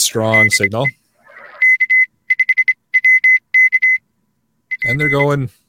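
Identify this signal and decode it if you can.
Morse code, heard as a keyed tone, about 26 wpm, spaced as for 21 wpm, 2050 Hz, with speech. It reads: DE K5WK P
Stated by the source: Live amateur radio HF contacts attempted on the air.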